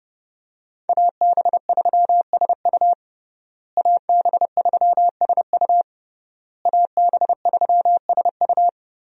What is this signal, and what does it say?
Morse code, heard as a keyed tone, about 32 wpm, 700 Hz. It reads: AB3SU AB3SU AB3SU
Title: USA Call Signs - 5 Characters - Repeated 3x - 30wpm